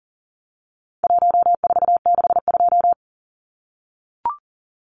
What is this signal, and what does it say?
Morse code, heard as a keyed tone, about 40 wpm, 700 Hz. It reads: 1462
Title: Numbers - 4-Digit 40wpm